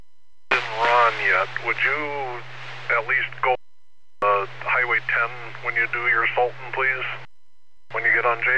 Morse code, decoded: none

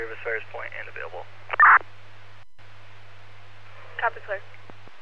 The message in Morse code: none

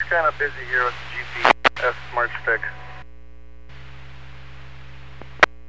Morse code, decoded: none

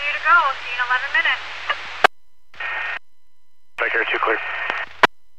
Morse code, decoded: none

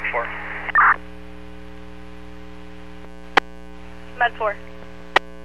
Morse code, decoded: none

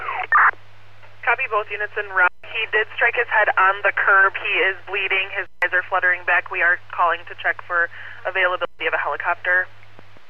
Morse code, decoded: none

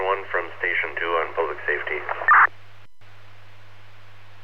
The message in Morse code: none

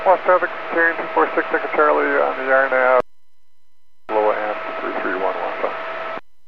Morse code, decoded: none

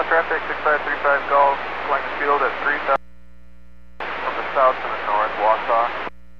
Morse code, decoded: none

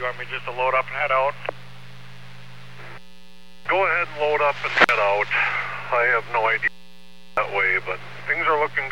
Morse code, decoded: none

